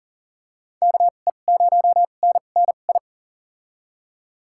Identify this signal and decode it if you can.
Morse code, heard as a keyed tone, about 40 wpm, spaced as for 29 wpm, 700 Hz. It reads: KE0NNI
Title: USA Call Signs - 2x Character Spacing - 40wpm